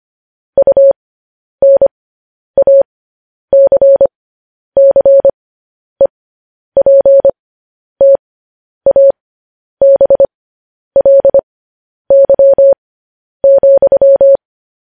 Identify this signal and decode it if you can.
Morse code, heard as a keyed tone, about 25 wpm, spaced as for 10 wpm, 550 Hz. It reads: UNACCEPTABLY,
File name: podcast / News headlines-1x-25x10